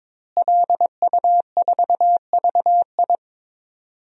Morse code, heard as a keyed tone, about 22 wpm, 700 Hz. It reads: LU4VI
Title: DXCC-100 Call Signs - 5 Characters - Repeated 2x - 22wpm